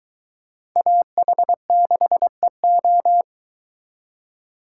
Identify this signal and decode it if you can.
Morse code, heard as a keyed tone, about 23 wpm, 700 Hz. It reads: AH6EO